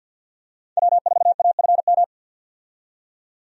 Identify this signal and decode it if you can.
Morse code, heard as a keyed tone, about 48 wpm, 700 Hz. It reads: W4AFM
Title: USA Call Signs - 5 Characters - 50wpm